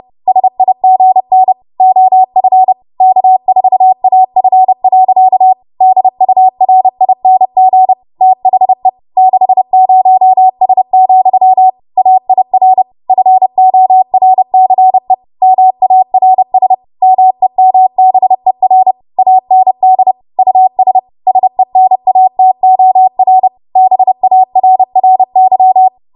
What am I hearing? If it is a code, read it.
Morse code, heard as a keyed tone, about 30 wpm, 750 Hz. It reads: SIGN OF K4AF. DURING THE 60S, AIR FORCE MARS MEMBER AND US SENATOR BARRY
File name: Archive / 30 WPM / 30WPM